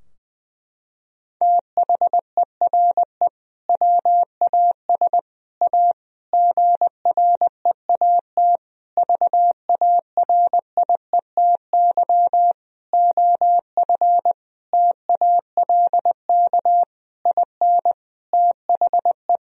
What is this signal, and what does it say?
Morse code, heard as a keyed tone, about 20 wpm, 700 Hz. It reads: THERE WAS A GREAT VARIETY OF TALK IN THE
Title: Sentences from Top 1000 Words - Review Part 4 - 20wpm